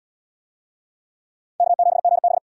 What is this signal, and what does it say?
Morse code, heard as a keyed tone, about 72 wpm, 700 Hz. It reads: K7GB